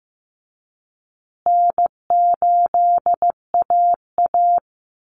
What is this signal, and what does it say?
Morse code, heard as a keyed tone, about 15 wpm, 700 Hz. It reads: N8AA
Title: Call Signs Repeated 2x - 15wpm